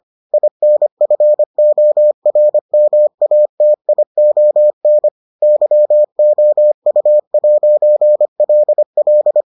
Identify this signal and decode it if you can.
Morse code, heard as a keyed tone, about 25 wpm, 600 Hz. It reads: INFORMATION YOU'LL